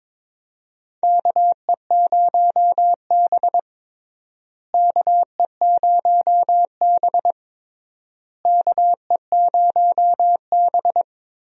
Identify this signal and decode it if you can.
Morse code, heard as a keyed tone, about 22 wpm, 700 Hz. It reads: KE0B KE0B KE0B
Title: USA Call Signs - 4 Characters - Repeated 3x - 22wpm